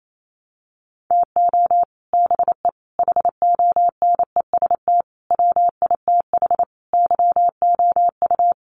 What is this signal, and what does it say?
Morse code, heard as a keyed tone, about 28 wpm, 690 Hz. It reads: TO BE HONEST WITH YOU
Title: Sets of 5 Words Encore 28wpm